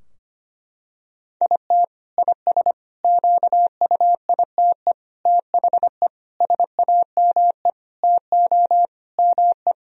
Morse code, heard as a keyed tone, about 25 wpm, 700 Hz. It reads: IT IS QUITE THE SAME TO ME
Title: Sentences from Top 1000 Words - Review Part 3 - 25wpm